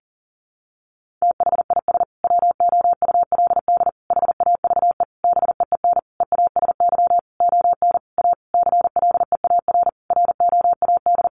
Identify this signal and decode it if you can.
Morse code, heard as a keyed tone, about 37 wpm, 700 Hz. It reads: THIS WOULD HAVE BEEN EASY ON A CLEAR ROAD
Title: Sentences from Top 800 Words 40wpm